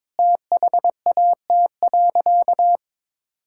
Morse code, heard as a keyed tone, about 22 wpm, 700 Hz. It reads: THAT.
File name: practice / 22wpm / Sentences with 4-Word Phrases - 22wpm